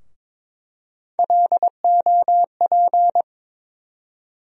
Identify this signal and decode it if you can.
Morse code, heard as a keyed tone, about 22 wpm, 700 Hz. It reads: LOP
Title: USA Call Sign Postfixes - 3 Letters 22wpm